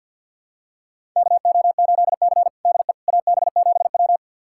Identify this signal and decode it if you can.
Morse code, heard as a keyed tone, about 51 wpm, 700 Hz. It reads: KO8C DE AB7W
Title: USA Callsign DE Callsign - 4 Characters - 50wpm